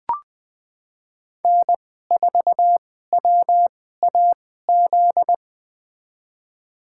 Morse code, heard as a keyed tone, about 21 wpm, spaced as for 15 wpm, 700 Hz. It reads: N4WAZ
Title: USA Call Signs - 5 Characters - 2x Character Spacing - 20wpm